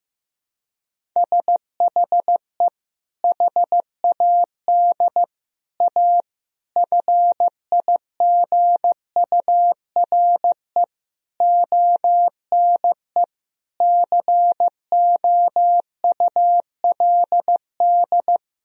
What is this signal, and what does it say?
Morse code, heard as a keyed tone, about 15 wpm, 700 Hz. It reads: SHE HAD A FIGURE ONE COULD
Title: Sentences from Top 1000 Words - Review Part 2 - 15wpm